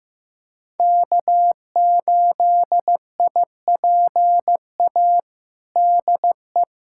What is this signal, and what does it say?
Morse code, heard as a keyed tone, about 15 wpm, 700 Hz. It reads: K8IPA DE
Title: USA Callsign DE Callsign - 5 Characters - 15wpm